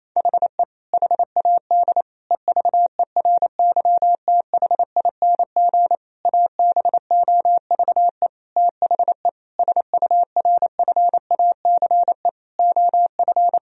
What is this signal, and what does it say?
Morse code, heard as a keyed tone, about 27 wpm, 700 Hz. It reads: HE HAD EVERYTHING ABOVE THE SURFACE OF
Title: Sentences from Top 1000 Words - Review Part 3 - 28wpm